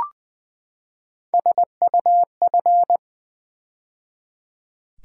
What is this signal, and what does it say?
Morse code, heard as a keyed tone, about 20 wpm, 700 Hz. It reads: SUF
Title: Build A Word - 3 Letters At A Time - 6-Letter Words - 20wpm